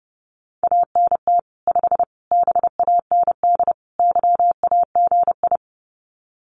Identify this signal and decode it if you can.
Morse code, heard as a keyed tone, about 30 wpm, 700 Hz. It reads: ANT 5 BAND YAGI
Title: Common QSO Phrases 30wpm